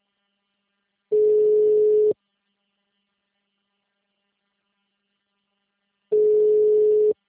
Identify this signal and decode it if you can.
Ringback tone (Europe, 425 Hz).